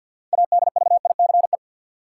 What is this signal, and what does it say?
Morse code, heard as a keyed tone, about 50 wpm, 700 Hz. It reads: ADVICE
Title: Sentences from Top 1100 Words - New Word Drill - 50wpm